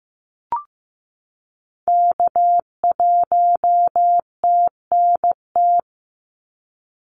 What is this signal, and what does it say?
Morse code, heard as a keyed tone, about 15 wpm, 700 Hz. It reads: K1TNT
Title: NA SOTA Activator Callsigns 15wpm